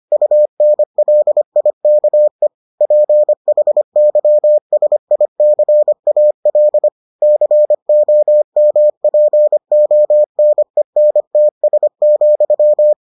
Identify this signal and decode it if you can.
Morse code, heard as a keyed tone, about 25 wpm, 600 Hz. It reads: UNLIKE PHYSICAL COMPONENTS,